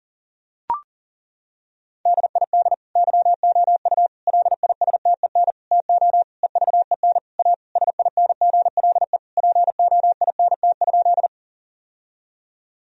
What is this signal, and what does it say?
Morse code, heard as a keyed tone, about 40 wpm, 700 Hz. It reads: DID YOU LISTEN TO EVEN A SINGLE POINT?
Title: Sentences from Top 1000 Words - Review Part 4 - 40wpm